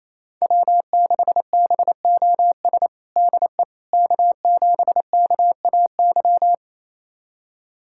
Morse code, heard as a keyed tone, about 28 wpm, 700 Hz. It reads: W6BOS DE K7KAY
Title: USA Callsign DE Callsign - 5 Characters - 28wpm